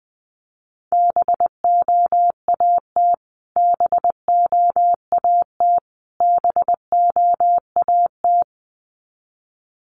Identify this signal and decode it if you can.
Morse code, heard as a keyed tone, about 20 wpm, 700 Hz. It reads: BOAT BOAT BOAT
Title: Top 500 Words Repeated 20wpm